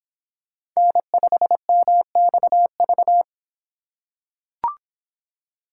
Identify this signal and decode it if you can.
Morse code, heard as a keyed tone, about 24 wpm, 700 Hz. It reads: N5MXV